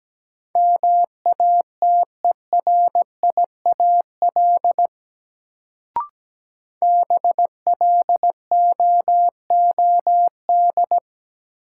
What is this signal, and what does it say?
Morse code, heard as a keyed tone, about 17 wpm, 700 Hz. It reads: MATERIAL BLOOD
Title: Sentences from Top 1100 Words - New Word Drill Alt - 17wpm